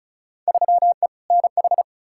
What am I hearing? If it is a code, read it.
Morse code, heard as a keyed tone, about 36 wpm, 700 Hz. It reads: 3E NH
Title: Field Day Exchanges Encore - 35wpm